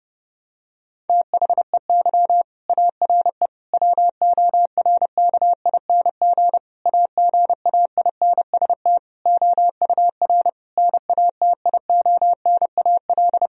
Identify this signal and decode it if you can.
Morse code, heard as a keyed tone, about 30 wpm, 700 Hz. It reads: THEY ARE WORKING AGAINST OUR NATIONAL
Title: Sentences from Top 1000 Words - Review Part 4 - 30wpm